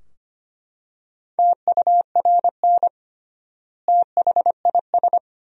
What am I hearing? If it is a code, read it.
Morse code, heard as a keyed tone, about 25 wpm, 700 Hz. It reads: TURN THIS
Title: 3x Word-Spacing - Sentences from Top 200 Words - 25wpm